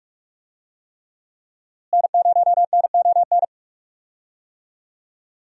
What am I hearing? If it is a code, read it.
Morse code, heard as a keyed tone, about 46 wpm, 700 Hz. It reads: N0NON
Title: USA Call Signs - 5 Characters - Repeated 3x - 45wpm